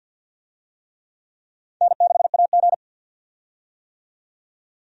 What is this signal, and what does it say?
Morse code, heard as a keyed tone, about 50 wpm, 700 Hz. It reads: N6AG
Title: USA Call Signs - 4 Characters - 50wpm